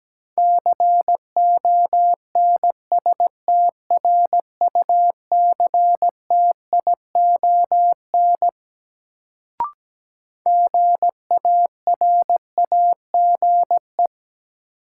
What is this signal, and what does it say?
Morse code, heard as a keyed tone, about 17 wpm, 700 Hz. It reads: CONSTRUCTION GARAGE
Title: Sentences from Top 1600 Words - New Word Drill - 17wpm